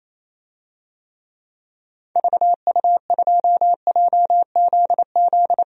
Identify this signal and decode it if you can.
Morse code, heard as a keyed tone, about 28 wpm, 700 Hz. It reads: VU2JZZ
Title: DXCC-100 Call Signs - 6 Characters - 28wpm